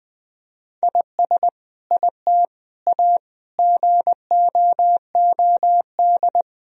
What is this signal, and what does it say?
Morse code, heard as a keyed tone, about 20 wpm, 700 Hz. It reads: IS IT A GOOD
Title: Sentences from Top 1000 Words - Review Part 4 - 20wpm